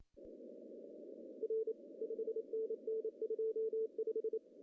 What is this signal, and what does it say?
Morse code, heard as a keyed tone, about 27 wpm, 440 Hz. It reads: R 5NN 25
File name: CW / cqww-cw-k5tr